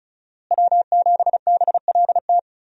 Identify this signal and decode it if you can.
Morse code, heard as a keyed tone, about 36 wpm, 700 Hz. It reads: W7BLT
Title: USA Call Signs - 5 Characters - Repeated 2x - 35wpm